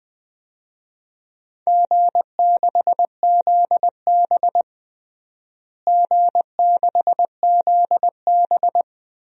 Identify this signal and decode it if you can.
Morse code, heard as a keyed tone, about 20 wpm, 700 Hz. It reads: G6ZB G6ZB
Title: DXCC-100 Call Signs - 4 Characters - Repeated 3x - 20wpm